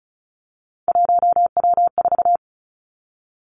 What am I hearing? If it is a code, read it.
Morse code, heard as a keyed tone, about 36 wpm, 700 Hz. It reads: 1W4
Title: Three Challenging Letter-Number Combinations 35wpm